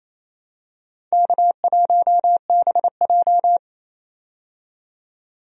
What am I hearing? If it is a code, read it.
Morse code, heard as a keyed tone, about 28 wpm, 700 Hz. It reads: K1BJ
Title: USA Call Signs - 4 Characters - 28wpm